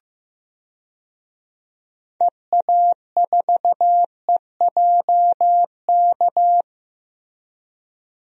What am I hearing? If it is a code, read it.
Morse code, heard as a keyed tone, about 15 wpm, 700 Hz. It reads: EA4EJK
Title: DXCC-100 Call Signs - 6 Characters - Repeated 2x - 15wpm